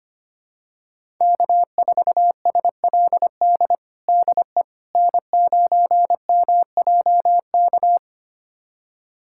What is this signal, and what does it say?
Morse code, heard as a keyed tone, about 25 wpm, 700 Hz. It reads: K4SLD DE N9MJK
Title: USA Callsign DE Callsign - 5 Characters - 25wpm